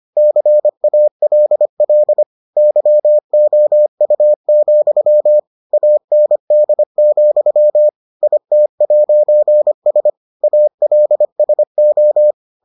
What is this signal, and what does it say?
Morse code, heard as a keyed tone, about 25 wpm, 600 Hz. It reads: CALL YOU, AND, IT'S ALSO